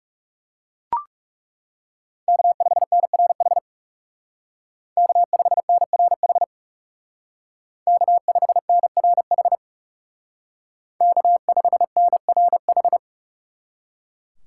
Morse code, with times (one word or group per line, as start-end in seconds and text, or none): 2.3-3.6 K5NRH
5.0-6.4 K5NRH
7.9-9.6 K5NRH
11.0-13.0 K5NRH